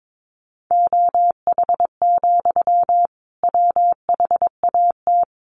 Morse code, heard as a keyed tone, about 22 wpm, 700 Hz. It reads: OH, WHAT
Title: Sentences from Top 1300 Words 22wpm